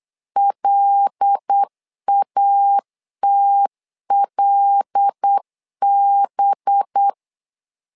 Morse code, heard as a keyed tone, about 8 wpm, 800 Hz. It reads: LATLB
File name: LEZIONI / 5 LBVC / 5 LBVC 8wpm 2935